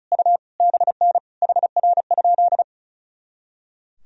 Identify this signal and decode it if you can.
Morse code, heard as a keyed tone, about 34 wpm, 700 Hz. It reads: U BN HR?
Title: Sentences from Top 200 Words and Abbreviations 35wpm